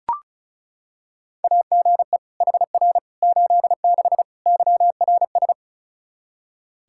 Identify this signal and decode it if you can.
Morse code, heard as a keyed tone, about 34 wpm, 700 Hz. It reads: AGE HR 86 YRS
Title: Practice QSO - Exchange 3 - More Info - 35wpm